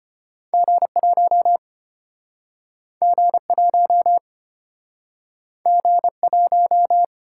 Morse code, heard as a keyed tone, about 35 wpm, 700 Hz. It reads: G1 G1 G1